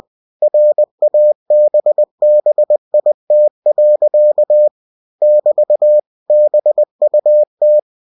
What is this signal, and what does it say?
Morse code, heard as a keyed tone, about 20 wpm, 600 Hz. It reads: RABBIT. <BT> BUT
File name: winnie-20wpm